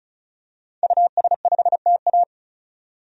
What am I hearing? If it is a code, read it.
Morse code, heard as a keyed tone, about 34 wpm, 700 Hz. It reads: US5TA
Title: DXCC-100 Call Signs - 5 Characters - 35wpm